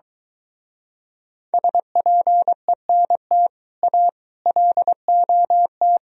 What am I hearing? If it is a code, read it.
Morse code, heard as a keyed tone, about 24 wpm, 700 Hz. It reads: SPENT A LOT